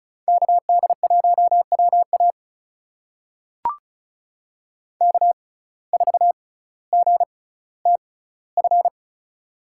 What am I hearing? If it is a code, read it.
Morse code, heard as a keyed tone, about 36 wpm, spaced as for 18 wpm, 700 Hz. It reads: KD1WA K4GTF
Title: USA Call Signs - 5 Characters - 6x Character Spacing - 35wpm